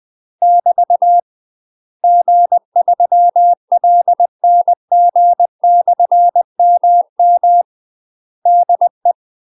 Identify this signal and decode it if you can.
Morse code, heard as a keyed tone, about 20 wpm, 700 Hz. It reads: <BT> G3LNG/MM DE
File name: UK MASTER QSOs TRAINING / 20WPM / UK MASTER QSOs 0031